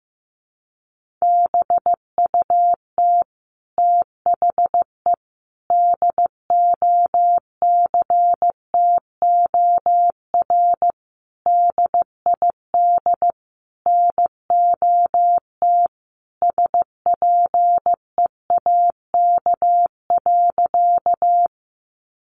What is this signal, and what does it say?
Morse code, heard as a keyed tone, about 15 wpm, 700 Hz. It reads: BUT THE DOCTOR DID NOT SPEAK.
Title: Sentences from Top 1500 Words 15wpm